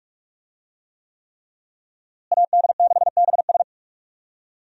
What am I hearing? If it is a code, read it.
Morse code, heard as a keyed tone, about 42 wpm, 700 Hz. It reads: AD6BS
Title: USA Call Signs - 5 Characters - Repeated 3x - 45wpm